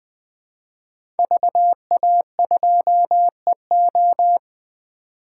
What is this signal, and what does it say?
Morse code, heard as a keyed tone, about 20 wpm, 700 Hz. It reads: VA2EO
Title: 2023 SOTA Chaser Call Signs - 20wpm